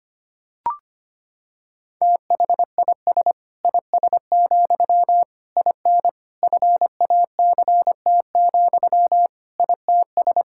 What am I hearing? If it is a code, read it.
Morse code, heard as a keyed tone, about 25 wpm, 700 Hz. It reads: THIS IS, IN FACT, ITS
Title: Sentences from Top 1000 Words - Review Part 5 - 25wpm